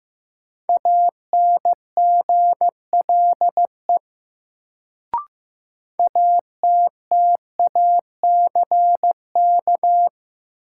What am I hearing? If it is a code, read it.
Morse code, heard as a keyed tone, about 15 wpm, 700 Hz. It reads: ANGLE ATTACK
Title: Sentences from Top 1500 Words - New Word Drill Alt - 15wpm